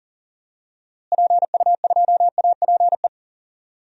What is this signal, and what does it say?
Morse code, heard as a keyed tone, about 40 wpm, 700 Hz. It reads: PU2APE
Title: DXCC-100 Call Signs - 6 Characters - Repeated 3x - 40wpm